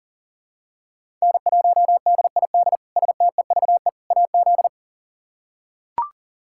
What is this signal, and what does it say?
Morse code, heard as a keyed tone, about 40 wpm, 700 Hz. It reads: N1DID STEVE AZ
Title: K1USN SST QSO Practice - 40wpm